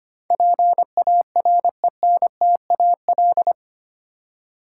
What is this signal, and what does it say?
Morse code, heard as a keyed tone, about 25 wpm, 700 Hz. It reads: PARENTAL
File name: Eight Letter Words - 25wpm